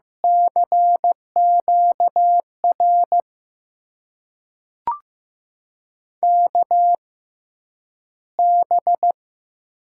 Morse code, heard as a keyed tone, about 15 wpm, spaced as for 9 wpm, 700 Hz. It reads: CQR KB
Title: USA Call Signs - 6 Characters - 6x Character Spacing - 15wpm